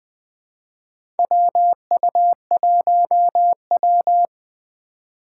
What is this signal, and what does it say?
Morse code, heard as a keyed tone, about 20 wpm, 700 Hz. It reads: WU1W
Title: USA Call Signs - 2x Character Spacing - 20wpm